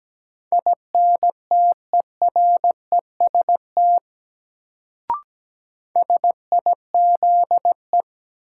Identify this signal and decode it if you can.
Morse code, heard as a keyed tone, about 17 wpm, 700 Hz. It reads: INTEREST SIZE
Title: Sentences from Top 400 Words - New Word Drill - 17wpm